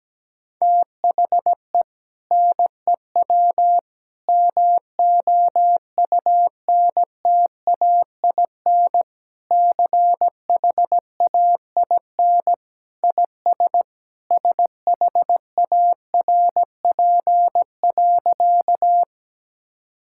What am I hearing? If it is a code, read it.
Morse code, heard as a keyed tone, about 17 wpm, 700 Hz. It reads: THE NEW MOUNTAIN CHAIN IS SHARP.
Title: Sentences from Top 1600 Words 17wpm